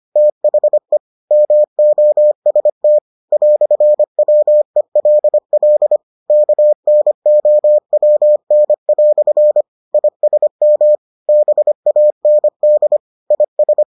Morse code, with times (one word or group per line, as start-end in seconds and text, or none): 0.2-1.0 THE
1.3-3.0 MOST
3.3-6.0 "WELL
6.3-9.6 KNOWN"
9.9-10.9 ISM
11.3-13.0 BAND
13.3-13.8 IS